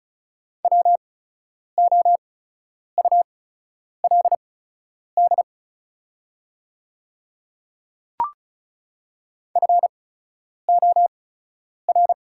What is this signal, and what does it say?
Morse code, heard as a keyed tone, about 36 wpm, spaced as for 10 wpm, 700 Hz. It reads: WOULD FOR
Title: 8x Character Spacing - Top 100 Words - 35wpm